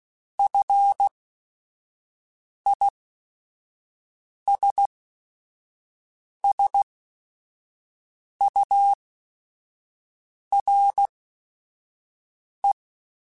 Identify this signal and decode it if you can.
Morse code, heard as a keyed tone, about 16 wpm, spaced as for 5 wpm, 800 Hz. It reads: FISSURE